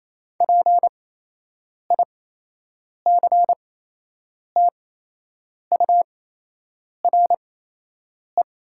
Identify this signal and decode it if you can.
Morse code, heard as a keyed tone, about 28 wpm, spaced as for 8 wpm, 700 Hz. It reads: PICTURE
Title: 8x Character Spacing - Top 200 Words - 28wpm